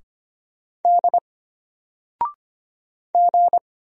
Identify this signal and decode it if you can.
Morse code, heard as a keyed tone, about 25 wpm, 700 Hz. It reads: D G
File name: Single-Letter-Number-25wpm